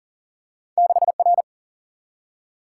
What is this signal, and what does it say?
Morse code, heard as a keyed tone, about 40 wpm, 700 Hz. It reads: 6R
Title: Single-Letter Single-Number 40wpm